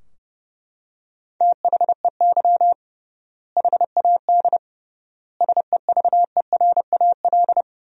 Morse code, heard as a keyed tone, about 30 wpm, 700 Hz. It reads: THEY HAD SEVERAL
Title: 3x Word-Spacing - Sentences from Top 400 Words - 30wpm